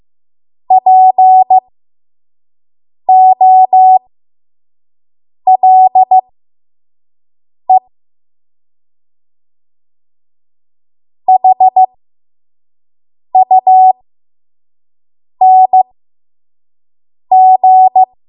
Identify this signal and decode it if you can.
Morse code, heard as a keyed tone, about 15 wpm, spaced as for 5 wpm, 750 Hz. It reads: POLE HUNG
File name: Morse / 05WPM